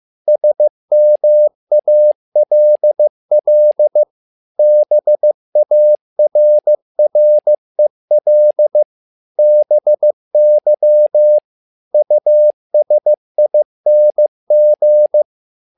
Morse code, heard as a keyed tone, about 15 wpm, 600 Hz. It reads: SMALL BARREL BY USING
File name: learning center / tower trouble / tower trouble 15wpm